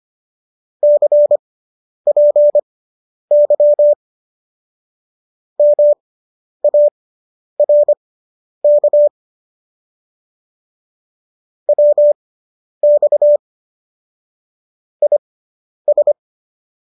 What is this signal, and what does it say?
Morse code, heard as a keyed tone, about 25 wpm, spaced as for 10 wpm, 590 Hz. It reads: CPY MARK WX IS